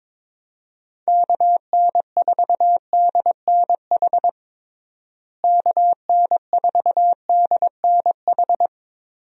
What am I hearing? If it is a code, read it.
Morse code, heard as a keyed tone, about 22 wpm, 700 Hz. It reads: KN4DNH KN4DNH
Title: USA Call Signs - 6 Characters - Repeated 3x - 22wpm